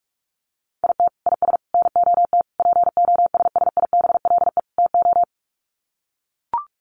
Morse code, heard as a keyed tone, about 44 wpm, 700 Hz. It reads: IT IS NOT POSSIBLE TO
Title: Sets of 5 Words Encore2 45wpm